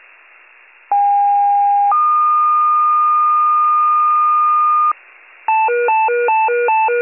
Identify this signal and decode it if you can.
Two-tone page, heard as a one-second tone, then a three-second tone, about 810 then 1220 Hz. It is heard from 0.9 to 4.9 s.
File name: wcalert